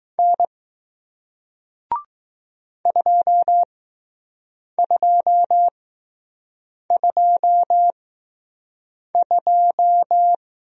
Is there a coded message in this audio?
Yes: Morse code from 0.2 to 10.3 s, heard as a keyed tone, about 20 wpm, spaced as for 6 wpm, 700 Hz. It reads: N 2222